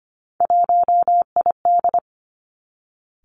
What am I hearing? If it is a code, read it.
Morse code, heard as a keyed tone, about 25 wpm, 700 Hz. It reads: 1ID
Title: Three Challenging Letter-Number Combinations 25wpm